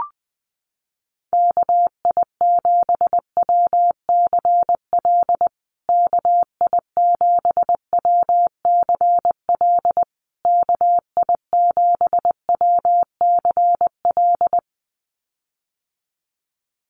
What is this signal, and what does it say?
Morse code, heard as a keyed tone, about 20 wpm, 700 Hz. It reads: KI7WCL KI7WCL KI7WCL
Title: USA Call Signs Repeated 3x 20wpm